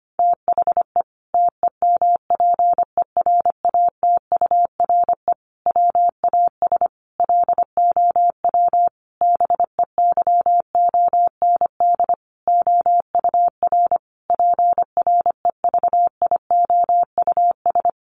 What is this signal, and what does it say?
Morse code, heard as a keyed tone, about 25 wpm, 700 Hz. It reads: THE TEMPERATURE WAS LOW BEYOND OUR PREVIOUS
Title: Sentences from Top 1500 Words 25wpm